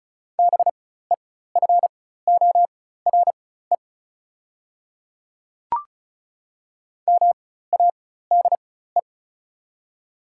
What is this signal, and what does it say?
Morse code, heard as a keyed tone, about 36 wpm, spaced as for 16 wpm, 700 Hz. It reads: BEFORE MADE
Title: 4x Character Spacing - Top 300 Words - 35wpm